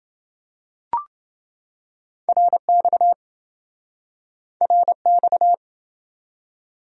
Morse code, heard as a keyed tone, about 28 wpm, 700 Hz. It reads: RX RX